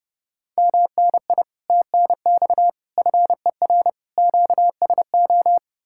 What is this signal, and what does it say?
Morse code, heard as a keyed tone, about 40 wpm, 700 Hz. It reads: MNI TNX FER QSO